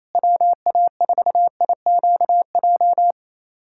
Morse code, heard as a keyed tone, about 28 wpm, 700 Hz. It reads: WA4IQJ
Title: USA Callsign DE Callsign - 6 Characters - 28wpm